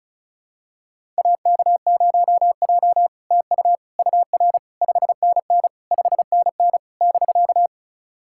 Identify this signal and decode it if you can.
Morse code, heard as a keyed tone, about 36 wpm, 700 Hz. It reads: AK0J TU UR 5NN 5NN <BK>
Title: SOTA QSO Practice 35wpm